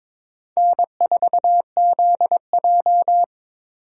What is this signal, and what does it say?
Morse code, heard as a keyed tone, about 22 wpm, 700 Hz. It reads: N4ZJ
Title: USA Call Signs - 4 Characters - 4x Character Spacing - 22wpm